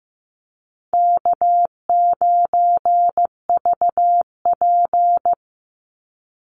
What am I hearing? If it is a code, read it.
Morse code, heard as a keyed tone, about 15 wpm, 700 Hz. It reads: K9VP
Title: Call Signs Repeated 2x - 15wpm